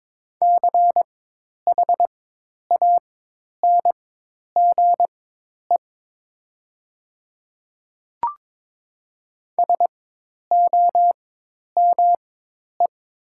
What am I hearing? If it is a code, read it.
Morse code, heard as a keyed tone, about 22 wpm, spaced as for 10 wpm, 700 Hz. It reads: CHANGE SOME
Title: 4x Character Spacing - Top 200 Words - 22wpm